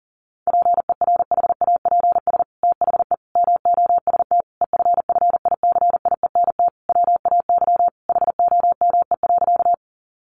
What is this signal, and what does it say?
Morse code, heard as a keyed tone, about 37 wpm, 700 Hz. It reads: PERHAPS THE MOST EFFICIENT WAY HOME.
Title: Sentences from Top 1300 Words 40wpm